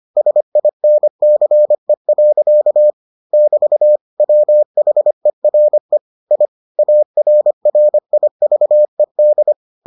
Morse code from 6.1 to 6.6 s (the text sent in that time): I